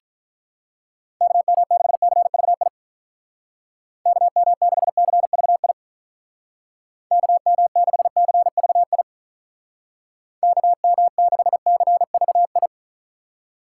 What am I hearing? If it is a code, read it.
Morse code, heard as a keyed tone, about 43 wpm, 700 Hz. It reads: KM6CVI KM6CVI KM6CVI KM6CVI
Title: Speed Racing - USA 2x3 Call Signs 35wpm